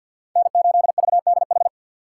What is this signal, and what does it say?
Morse code, heard as a keyed tone, about 53 wpm, 700 Hz. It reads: N8VDH
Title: USA Call Signs - 6x Character Spacing - 50wpm